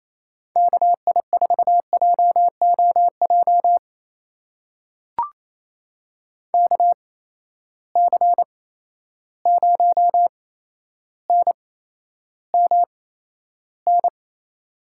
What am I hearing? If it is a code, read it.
Morse code, heard as a keyed tone, about 28 wpm, spaced as for 12 wpm, 700 Hz. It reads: KI4JOJ KC0NMN
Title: USA Call Signs - 6 Characters - 8x Character Spacing - 28wpm